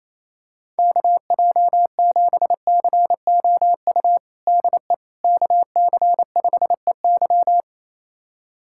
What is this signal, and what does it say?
Morse code, heard as a keyed tone, about 28 wpm, 700 Hz. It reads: KJ7COU DE KC5EY
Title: USA Callsign DE Callsign 28wpm